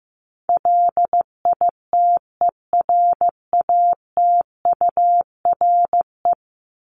Morse code, heard as a keyed tone, about 15 wpm, 700 Hz. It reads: LITERATURE
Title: Top 501-750 Words 15wpm